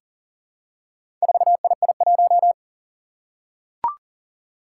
Morse code, heard as a keyed tone, about 40 wpm, 700 Hz. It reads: 4II1
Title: Four Challenging Letter-Number Combinations 40wpm